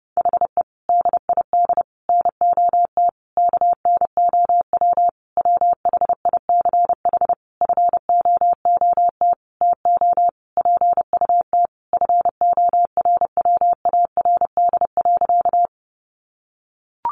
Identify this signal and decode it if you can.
Morse code, heard as a keyed tone, about 30 wpm, 700 Hz. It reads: HE DID NOT KNOW WHICH FOOT TO PUT FORWARD.
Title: Sentences from Top 1300 Words 30wpm